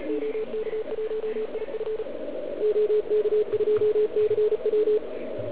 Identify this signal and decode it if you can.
Morse code, heard as a keyed tone, about 36 wpm, 440 Hz, with radio noise. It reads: OK1IF OK2CW